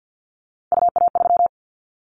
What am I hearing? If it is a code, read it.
Morse code, heard as a keyed tone, about 50 wpm, 700 Hz. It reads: UA3
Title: Callsign Prefixes for Top 50 DXCC - 50wpm